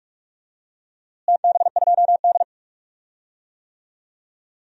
Morse code, heard as a keyed tone, about 42 wpm, 700 Hz. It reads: TB2D
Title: DXCC-100 Call Signs - 4 Characters - Repeated 2x - 45wpm